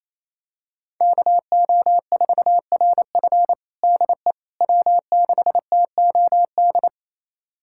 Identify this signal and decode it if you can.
Morse code, heard as a keyed tone, about 28 wpm, 700 Hz. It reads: KO4RF DE W6TOD
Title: USA Callsign DE Callsign - 5 Characters - 28wpm